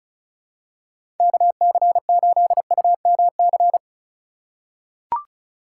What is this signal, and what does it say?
Morse code, heard as a keyed tone, about 36 wpm, 700 Hz. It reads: KC8UMC
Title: USA Call Signs - 6 Characters - 2x Character Spacing - 35wpm